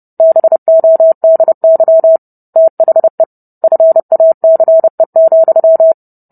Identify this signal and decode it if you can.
Morse code, heard as a keyed tone, about 30 wpm, 650 Hz. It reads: BODY THE FACE,